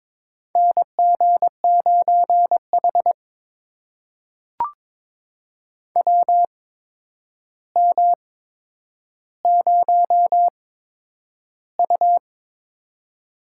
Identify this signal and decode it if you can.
Morse code, heard as a keyed tone, about 22 wpm, spaced as for 9 wpm, 700 Hz. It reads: NG9H WM0U
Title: USA Call Signs - 4 Characters - 8x Character Spacing - 22wpm